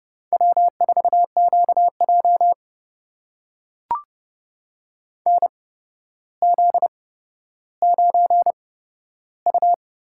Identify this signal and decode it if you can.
Morse code, heard as a keyed tone, about 30 wpm, spaced as for 13 wpm, 700 Hz. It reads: W4QJ NZ9U
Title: USA Call Signs - 8x Character Spacing - 30wpm